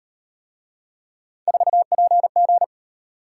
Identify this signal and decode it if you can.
Morse code, heard as a keyed tone, about 37 wpm, 700 Hz. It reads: 4PG